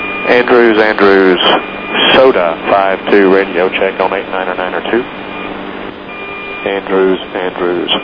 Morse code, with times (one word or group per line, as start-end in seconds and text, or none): none